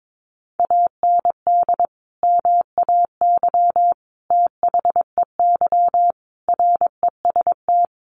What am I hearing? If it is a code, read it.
Morse code, heard as a keyed tone, about 22 wpm, 700 Hz. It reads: AND MAY THEY REST
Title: Sentences from Top 1500 Words 22wpm